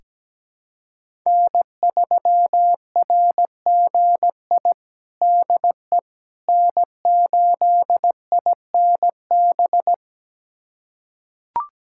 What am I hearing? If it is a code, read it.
Morse code, heard as a keyed tone, about 17 wpm, 700 Hz. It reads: N3RGI DE N8INB
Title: USA Callsign DE Callsign - 5 Characters - 17wpm